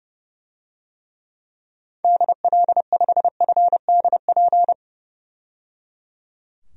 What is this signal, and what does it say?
Morse code, heard as a keyed tone, about 30 wpm, 700 Hz. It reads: DL5FDP
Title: DXCC-100 Call Signs - 6 Characters - 30wpm